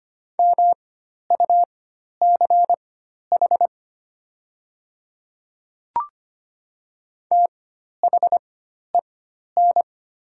Morse code, heard as a keyed tone, about 25 wpm, spaced as for 12 wpm, 700 Hz. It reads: MUCH THEN
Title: 4x Character Spacing - Top 200 Words - 25wpm